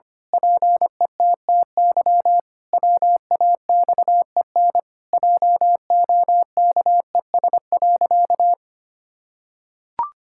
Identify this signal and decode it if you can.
Morse code, heard as a keyed tone, about 25 wpm, 700 Hz. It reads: PETTY WAXEN JOKES.
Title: Pangram Challenge v2 - 25wpm